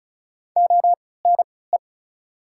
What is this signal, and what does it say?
Morse code, heard as a keyed tone, about 36 wpm, spaced as for 20 wpm, 690 Hz. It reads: ONE
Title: 3x Character Spacing - Top 500 Words - 35wpm